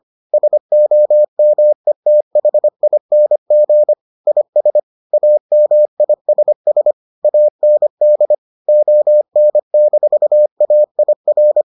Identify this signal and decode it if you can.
Morse code, heard as a keyed tone, about 25 wpm, 600 Hz. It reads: SOMETHING IS AMISS AND ON-AIR